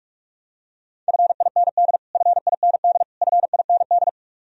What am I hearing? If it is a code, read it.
Morse code, heard as a keyed tone, about 46 wpm, 700 Hz. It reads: FIND FIND FIND
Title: Top 100 Words Repeated 45wpm